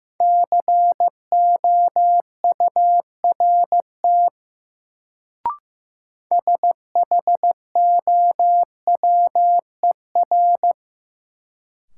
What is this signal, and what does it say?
Morse code, heard as a keyed tone, about 15 wpm, 700 Hz. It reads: COURT SHOWER
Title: Sentences from Top 1600 Words - New Word Drill - 15wpm